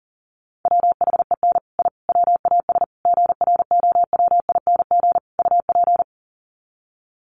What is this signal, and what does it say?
Morse code, heard as a keyed tone, about 40 wpm, 700 Hz. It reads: WHEN I WAS GROWING UP
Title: Sets of 5 Words Encore 40wpm